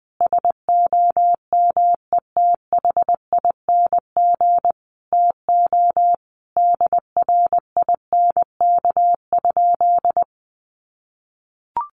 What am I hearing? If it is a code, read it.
Morse code, heard as a keyed tone, about 20 wpm, 700 Hz. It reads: SOMETHING TO DRINK?
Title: Sentences from Top 1500 Words 20wpm